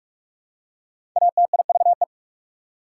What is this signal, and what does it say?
Morse code, heard as a keyed tone, about 47 wpm, 700 Hz. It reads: ATIVE
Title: Prefixes and Suffixes 45wpm